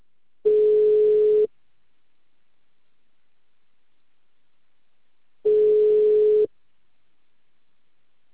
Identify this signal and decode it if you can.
Ringback tone (Europe, 425 Hz).
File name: out- -2044- - - .52074